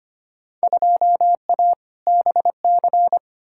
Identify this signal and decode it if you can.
Morse code, heard as a keyed tone, about 25 wpm, 700 Hz. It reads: 2A BC
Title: Field Day Exchanges - 25wpm